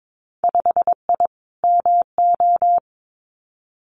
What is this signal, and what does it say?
Morse code, heard as a keyed tone, about 22 wpm, 700 Hz. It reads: 5I MO
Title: Winter Field Day Exchanges Encore 22wpm